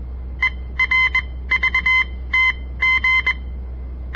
Morse code, heard as a keyed tone, about 21 wpm, spaced as for 16 wpm, 2050 Hz. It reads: ERVTG
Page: On the air Morse code practice from the Southern Montana Amatuer Radio Association; all code, no read back